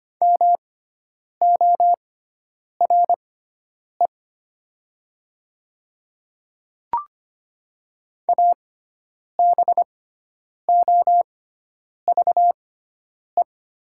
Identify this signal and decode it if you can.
Morse code, heard as a keyed tone, about 25 wpm, spaced as for 9 wpm, 700 Hz. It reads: MORE ABOVE
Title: 6x Character Spacing - Top 500 Words - 25wpm